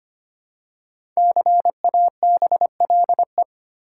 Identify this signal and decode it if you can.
Morse code, heard as a keyed tone, about 25 wpm, 700 Hz. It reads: CABLE
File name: 5 Letter Words - First 3 Letters Given - 25wpm